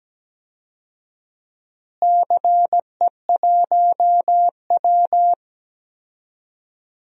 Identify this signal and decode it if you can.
Morse code, heard as a keyed tone, about 17 wpm, 700 Hz. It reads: CE1W
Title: DXCC-100 Call Signs - 4 Characters - Repeated 2x - 17wpm